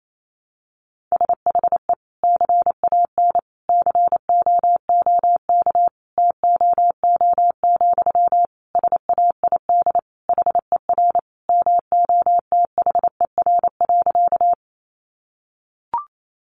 Morse code, heard as a keyed tone, about 28 wpm, 700 Hz. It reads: SHE CAN COOK TOO, SAID HER MOTHER.
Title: Sentences from Top 700 Words 28wpm